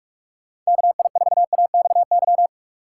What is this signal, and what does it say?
Morse code, heard as a keyed tone, about 46 wpm, 700 Hz. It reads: KI4AXY
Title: USA Call Signs - 6 Characters - 8x Character Spacing - 45wpm